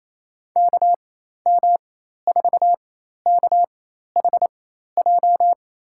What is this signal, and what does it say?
Morse code, heard as a keyed tone, about 28 wpm, spaced as for 13 wpm, 700 Hz. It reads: KM4KHJ
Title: USA Call Signs - 6 Characters - 4x Character Spacing - 28wpm